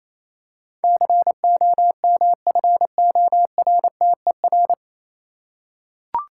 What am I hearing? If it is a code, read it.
Morse code, heard as a keyed tone, about 27 wpm, 690 Hz. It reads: COMFORTER
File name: Easy Words - Speed-Racing with Hints - 22wpm